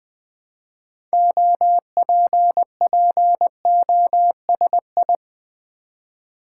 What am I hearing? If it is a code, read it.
Morse code, heard as a keyed tone, about 21 wpm, 700 Hz. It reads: OPPOSI
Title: Build A Word - 2 Letters At A Time - 8-Letter Words - 20wpm